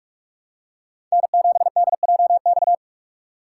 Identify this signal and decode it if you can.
Morse code, heard as a keyed tone, about 44 wpm, 700 Hz. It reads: N7DJX
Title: USA Call Signs - 5 Characters - 4x Character Spacing - 45wpm